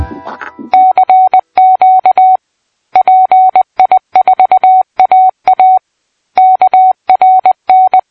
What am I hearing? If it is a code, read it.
Morse code, heard as a keyed tone, about 20 wpm, 720 Hz. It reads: CQ PI4AA KRN
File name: PI4AA Uitzending